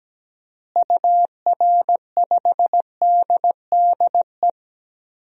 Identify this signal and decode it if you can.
Morse code, heard as a keyed tone, about 17 wpm, 700 Hz. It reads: UR5DDE
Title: DXCC-100 Call Signs - 6 Characters - Repeated 3x - 17wpm